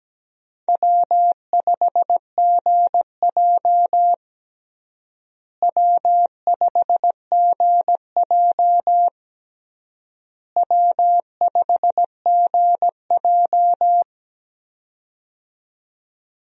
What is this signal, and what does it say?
Morse code, heard as a keyed tone, about 17 wpm, 700 Hz. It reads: W5GJ W5GJ W5GJ
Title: USA Call Signs - 4 Characters - Repeated 3x - 17wpm